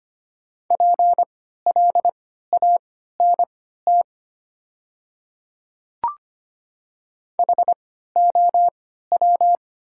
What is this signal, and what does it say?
Morse code, heard as a keyed tone, about 25 wpm, spaced as for 14 wpm, 700 Hz. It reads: PLANT HOW
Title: 3x Character Spacing - Top 300 Words - 25wpm